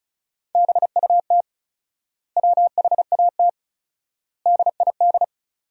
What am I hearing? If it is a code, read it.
Morse code, heard as a keyed tone, about 34 wpm, 700 Hz. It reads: BUT WHAT DID
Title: 4x Word-Spacing - Sentences from Top 100 Words - 35wpm